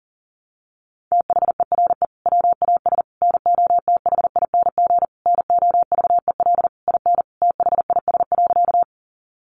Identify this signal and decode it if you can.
Morse code, heard as a keyed tone, about 40 wpm, 700 Hz. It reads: THERE WAS NOTHING NOVEL IN THIS.
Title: Sentences from Top 1300 Words 40wpm